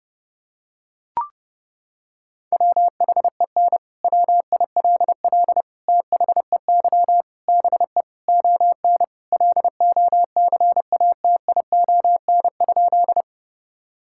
Morse code, heard as a keyed tone, about 32 wpm, 700 Hz. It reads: WHEN WILL THEY BE ON LOCATION?
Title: Sentences from Top 1000 Words - Review Part 6 - 30wpm